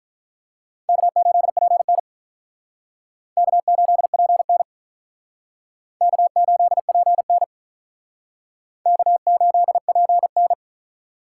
Morse code, heard as a keyed tone, about 45 wpm, 700 Hz. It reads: K8PN K8PN K8PN K8PN